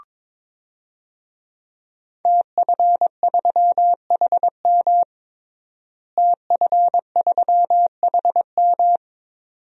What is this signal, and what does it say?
Morse code, heard as a keyed tone, about 22 wpm, 700 Hz. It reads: TF3HM TF3HM
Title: DXCC-100 Call Signs - 5 Characters - Repeated 2x - 22wpm